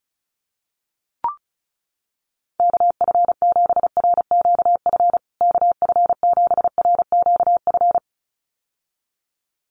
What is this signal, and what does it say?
Morse code, heard as a keyed tone, about 34 wpm, 700 Hz. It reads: KF7RQF KF7RQF
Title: USA Call Signs Repeated 2x 35wpm